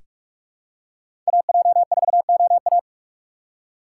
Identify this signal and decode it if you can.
Morse code, heard as a keyed tone, about 46 wpm, 700 Hz. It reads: AJ4OA
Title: USA Call Signs - 5 Characters - 2x Character Spacing - 45wpm